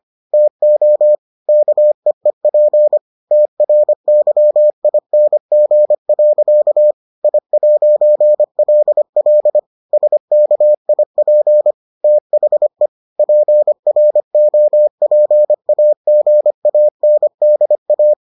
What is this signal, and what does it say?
Morse code, heard as a keyed tone, about 25 wpm, 600 Hz. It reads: TO KEEP TRYING. I'LL SKIP THE PROPAGANDA